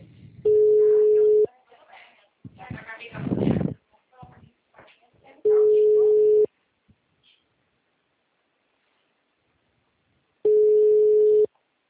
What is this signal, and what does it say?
Ringback tone (Europe, 425 Hz), with speech.